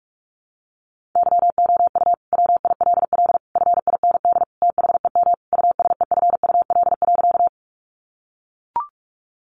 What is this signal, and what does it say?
Morse code, heard as a keyed tone, about 44 wpm, 700 Hz. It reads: YOU WILL FIND THEM USEFUL.
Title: Sentences from Top 700 Words 45wpm